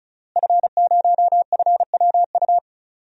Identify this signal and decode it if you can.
Morse code, heard as a keyed tone, about 36 wpm, 700 Hz. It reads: F0FWU
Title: DXCC-100 Call Signs - 5 Characters - Repeated 2x - 35wpm